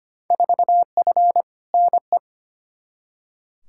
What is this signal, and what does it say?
Morse code, heard as a keyed tone, about 25 wpm, 700 Hz. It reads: 4F NE
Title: Field Day Exchanges Encore - 25wpm